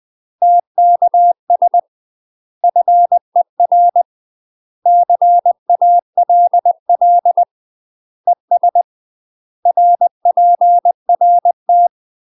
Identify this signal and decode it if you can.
Morse code, heard as a keyed tone, about 20 wpm, 700 Hz. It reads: TKS FER CALL ES RPRT
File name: UK MASTER QSOs TRAINING/20WPM